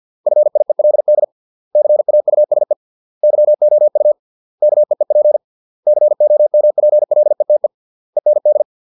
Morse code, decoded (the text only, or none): FIELD CAUSE YOU KEEP COMPLETE END